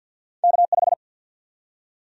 Morse code, heard as a keyed tone, about 53 wpm, 700 Hz. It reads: K5
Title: USA Call Sign Prefixes - 50wpm